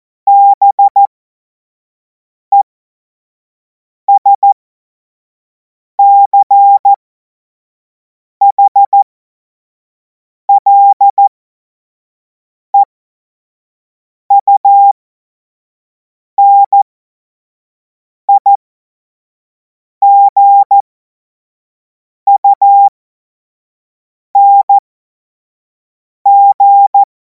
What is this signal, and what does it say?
Morse code, heard as a keyed tone, about 14 wpm, spaced as for 5 wpm, 800 Hz. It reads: BESCHLEUNIGUNG